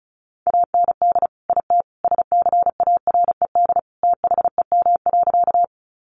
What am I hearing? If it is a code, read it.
Morse code, heard as a keyed tone, about 34 wpm, 700 Hz. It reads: AND IT SCARED THEM.
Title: Sentences from Top 700 Words 35wpm